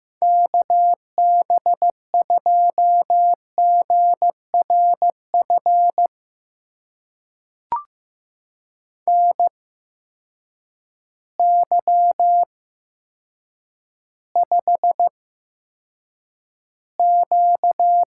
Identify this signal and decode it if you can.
Morse code, heard as a keyed tone, about 15 wpm, spaced as for 7 wpm, 700 Hz. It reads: KB2GRF NY5Q
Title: USA Call Signs - 8x Character Spacing - 15wpm